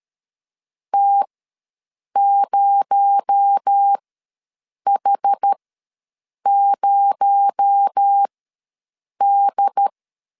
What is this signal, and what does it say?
Morse code, heard as a keyed tone, about 13 wpm, spaced as for 7 wpm, 800 Hz. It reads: T0H0D
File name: LEZIONI / 6 PKX0 / 6 PKX0 ws 12wpm 3448